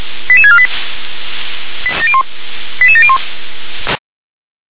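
Five-tone selective call (ZVEI), heard as quick runs of five tones, tones of about 70 ms. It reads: 88648 88701
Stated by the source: ZVEI five-tone call received on SDR.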